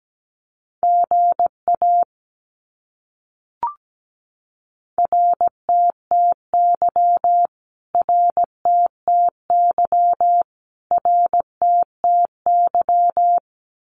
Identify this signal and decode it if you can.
Morse code, heard as a keyed tone, about 17 wpm, 700 Hz. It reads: GA RTTY RTTY RTTY
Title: Abbreviations Repeated 3x - 17wpm